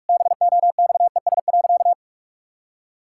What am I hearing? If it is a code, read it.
Morse code, heard as a keyed tone, about 46 wpm, 700 Hz. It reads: BOXES.
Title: Pangram Challenge v2 - 45wpm